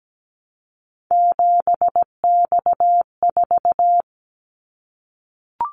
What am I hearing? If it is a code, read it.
Morse code, heard as a keyed tone, about 17 wpm, 700 Hz. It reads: 7X4
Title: Callsign Prefixes for Top 100 DXCC - 17wpm